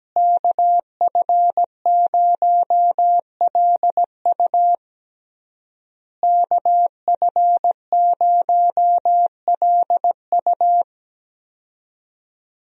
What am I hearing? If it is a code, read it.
Morse code, heard as a keyed tone, about 17 wpm, 700 Hz. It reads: KF0LU KF0LU
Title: USA Call Signs - 5 Characters - Repeated 3x - 17wpm